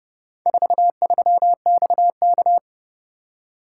Morse code, heard as a keyed tone, about 31 wpm, 700 Hz. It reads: 43XK